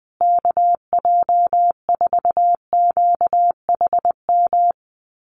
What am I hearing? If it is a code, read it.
Morse code, heard as a keyed tone, about 20 wpm, 700 Hz. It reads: KJ4QHM
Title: USA Call Signs Repeated 2x 20wpm